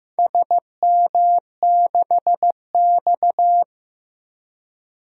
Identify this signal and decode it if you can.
Morse code, heard as a keyed tone, about 15 wpm, 700 Hz. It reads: SM6X
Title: DXCC-100 Call Signs - 4 Characters - Repeated 2x - 15wpm